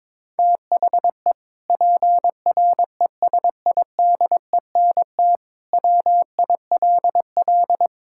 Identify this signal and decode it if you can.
Morse code, heard as a keyed tone, about 22 wpm, 700 Hz. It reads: THE PRESIDENT WILL
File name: practice / 22wpm / Sentences with 2-Word Phrases - 22wpm